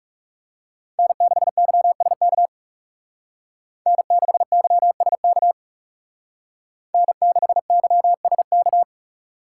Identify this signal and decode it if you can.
Morse code, heard as a keyed tone, about 37 wpm, 700 Hz. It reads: N6YSK N6YSK N6YSK
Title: Speed Racing - USA 1x3 Call Signs 30wpm